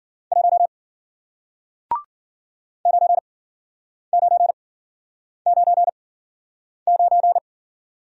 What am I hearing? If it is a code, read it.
Morse code, heard as a keyed tone, about 51 wpm, spaced as for 9 wpm, 700 Hz. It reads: 1 9999